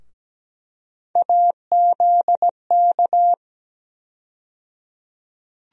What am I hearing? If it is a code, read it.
Morse code, heard as a keyed tone, about 17 wpm, 700 Hz. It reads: AZK
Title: USA Call Sign Postfixes - 3 Letters 17wpm